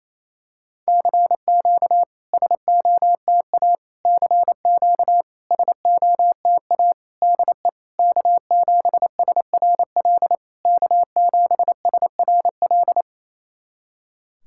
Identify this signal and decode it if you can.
Morse code, heard as a keyed tone, about 28 wpm, 700 Hz. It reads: CQ SOTA CQ SOTA DE K7SRL K7SRL
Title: SOTA QSO Practice 28wpm